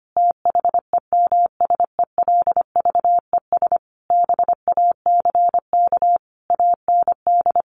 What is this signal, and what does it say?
Morse code, heard as a keyed tone, about 25 wpm, 700 Hz. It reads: THEMSELVES BACK AND
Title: A Sound Of Thunder 25wpm